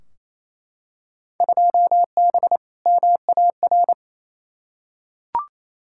Morse code, heard as a keyed tone, about 28 wpm, 700 Hz. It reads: 2B MAR
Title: Field Day Exchanges Encore - 28wpm